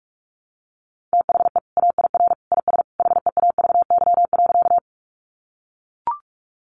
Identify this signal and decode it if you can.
Morse code, heard as a keyed tone, about 44 wpm, 700 Hz. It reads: THE AIR IS HEAVY.
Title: Sentences from Top 1100 Words 45wpm